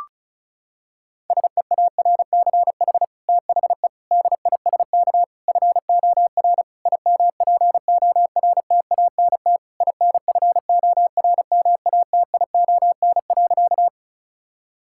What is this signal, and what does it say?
Morse code, heard as a keyed tone, about 36 wpm, 700 Hz. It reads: SEARCH THE DISK FOR IMPORTANT INFORMATION.
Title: Sentences from Top 1000 Words - Review Part 6 - 35wpm